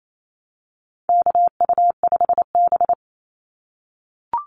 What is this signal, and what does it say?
Morse code, heard as a keyed tone, about 28 wpm, 700 Hz. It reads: KU5B
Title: Call Signs - 28wpm